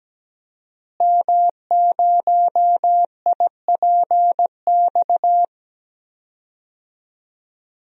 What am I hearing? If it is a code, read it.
Morse code, heard as a keyed tone, about 17 wpm, 700 Hz. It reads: M0IPX
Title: DXCC-100 Call Signs - 5 Characters - Repeated 2x - 17wpm